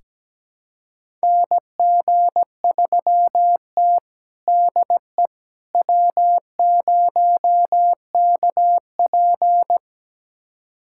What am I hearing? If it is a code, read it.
Morse code, heard as a keyed tone, about 17 wpm, 700 Hz. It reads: NG3T DE W0KP
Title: USA Callsign DE Callsign - 4 Characters - 17wpm